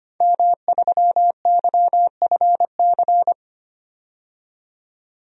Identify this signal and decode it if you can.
Morse code, heard as a keyed tone, about 25 wpm, 700 Hz. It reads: M3YFC
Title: DXCC-100 Call Signs - 5 Characters - 25wpm